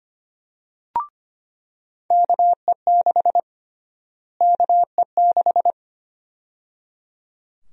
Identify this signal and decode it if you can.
Morse code, heard as a keyed tone, about 25 wpm, 700 Hz. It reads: KE6 KE6
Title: USA Call Sign Prefixes - Repeated 2x - 25wpm